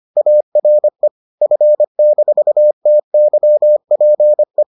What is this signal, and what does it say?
Morse code, heard as a keyed tone, about 25 wpm, 600 Hz. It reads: ARE F-TYPE